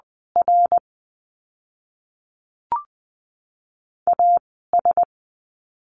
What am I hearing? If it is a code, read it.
Morse code, heard as a keyed tone, about 20 wpm, 700 Hz. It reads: R A S
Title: 101 Lesson 17f - Letter u - Words 20wpm